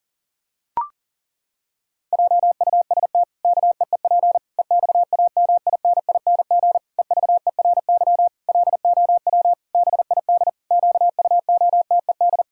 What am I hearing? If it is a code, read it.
Morse code, heard as a keyed tone, about 40 wpm, 700 Hz. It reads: JUST KEEP EXAMINING EVERY LOW BID QUOTED
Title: Pangram Challenge v2 - 40wpm